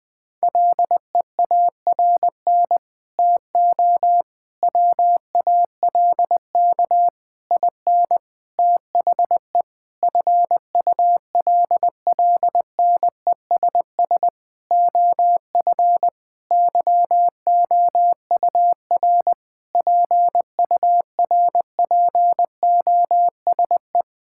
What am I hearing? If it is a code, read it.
Morse code, heard as a keyed tone, about 20 wpm, 700 Hz. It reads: LEARN TO WALK IN THE FULLNESS OF YOUR PURPOSE
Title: Sentences with 9-Word Phrases - 20wpm